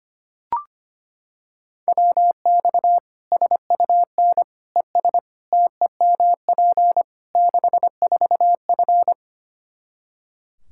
Morse code, heard as a keyed tone, about 25 wpm, 700 Hz. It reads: WX SUN ES TEMP 64F
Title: Practice QSO - Exchange 2 - Info - 25wpm